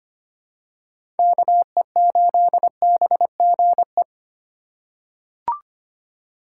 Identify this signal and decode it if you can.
Morse code, heard as a keyed tone, about 25 wpm, 700 Hz. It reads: KE8BGE
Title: USA Call Signs - 6 Characters - 3x Character Spacing - 25wpm